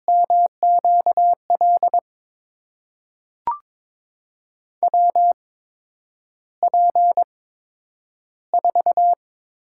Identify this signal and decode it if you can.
Morse code, heard as a keyed tone, about 22 wpm, spaced as for 9 wpm, 700 Hz. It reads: MQL WP4
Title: USA Call Signs - 6 Characters - 8x Character Spacing - 22wpm